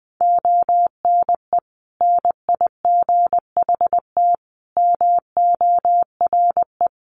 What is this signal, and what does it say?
Morse code, heard as a keyed tone, about 20 wpm, 700 Hz. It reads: ONE NIGHT MORE
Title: Sentences from Top 300 Words 20wpm